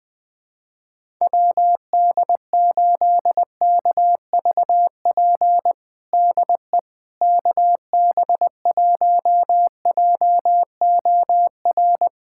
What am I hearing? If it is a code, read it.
Morse code, heard as a keyed tone, about 20 wpm, 700 Hz. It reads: WD8KVP DE KB1JOR
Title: USA Callsign DE Callsign - 6 Characters - 20wpm